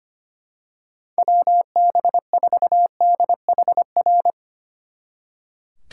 Morse code, heard as a keyed tone, about 27 wpm, 700 Hz. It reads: WB4DHR
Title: Speed Racing - USA 2x3 Call Signs 25wpm